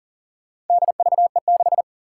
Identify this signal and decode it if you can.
Morse code, heard as a keyed tone, about 40 wpm, 700 Hz. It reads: DVE6
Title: Four Challenging Letter-Number Combinations 40wpm